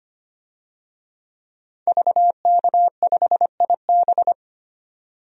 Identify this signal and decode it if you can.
Morse code, heard as a keyed tone, about 25 wpm, 700 Hz. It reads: VK5IB
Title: DXCC-100 Call Signs - 5 Characters - 25wpm